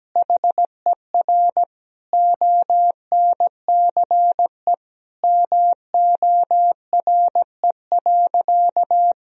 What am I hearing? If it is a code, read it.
Morse code, heard as a keyed tone, about 17 wpm, 700 Hz. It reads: HER ONCE MORE.
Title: Sentences from Top 500 Words and Abbreviations 17wpm